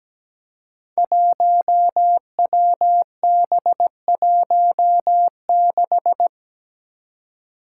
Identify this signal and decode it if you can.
Morse code, heard as a keyed tone, about 17 wpm, 700 Hz. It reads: 1WB16
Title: Five Challenging Letter-Number Combinations 17wpm